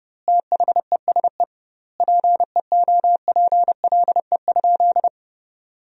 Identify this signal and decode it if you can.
Morse code, heard as a keyed tone, about 30 wpm, 700 Hz. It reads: THESE PEOPLE?
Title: 2x Word-Spacing - Sentences from Top 200 Words - 30wpm